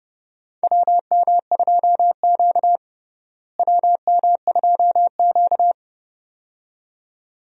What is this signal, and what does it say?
Morse code, heard as a keyed tone, about 30 wpm, 700 Hz. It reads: WM2Q WM2Q
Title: USA Call Signs - 4 Characters - Repeated 2x - 30wpm